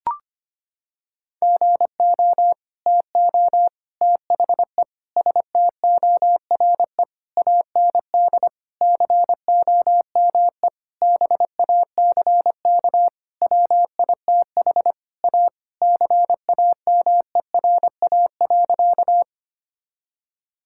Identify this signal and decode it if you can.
Morse code, heard as a keyed tone, about 25 wpm, 700 Hz. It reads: GO TO THE STORE AND COME BACK WITH A CAMERA.
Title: Sentences from Top 1000 Words - Review Part 5 - 25wpm